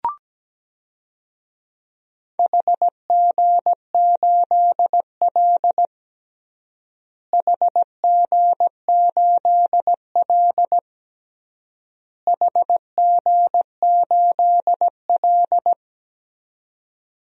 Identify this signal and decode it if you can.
Morse code, heard as a keyed tone, about 17 wpm, 700 Hz. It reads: HG8L HG8L HG8L
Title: DXCC-100 Call Signs - 4 Characters - Repeated 3x - 17wpm